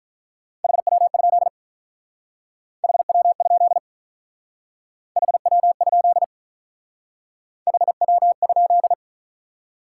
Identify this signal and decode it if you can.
Morse code, heard as a keyed tone, about 43 wpm, 700 Hz. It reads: HW? HW? HW? HW?
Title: Speed-Racing Top 90 QSO Elements 35wpm